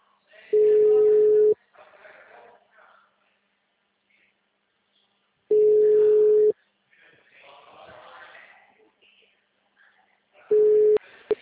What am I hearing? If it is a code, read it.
Ringback tone (Europe, 425 Hz).
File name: out- -2016- - - .11